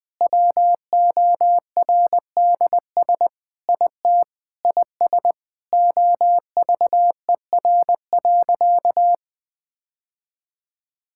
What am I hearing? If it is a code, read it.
Morse code, heard as a keyed tone, about 19 wpm, 700 Hz. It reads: WORDS IT IS OVER.
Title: Sentences with 3-Word Phrases - Spoken Beginning - 20wpm